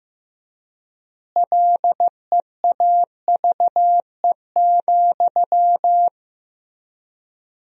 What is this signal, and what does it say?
Morse code, heard as a keyed tone, about 15 wpm, 700 Hz. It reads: LEAVE,
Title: 3x Word-Spacing - Sentences from Top 400 Words - 15wpm